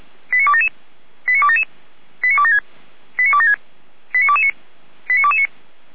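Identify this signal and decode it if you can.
Five-tone selective call (ZVEI), heard as quick runs of five tones, tones of about 70 ms. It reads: 89277 89277 89276 89276 89229 89229
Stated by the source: ZVEI five-tone selective call 89310 received via SDR.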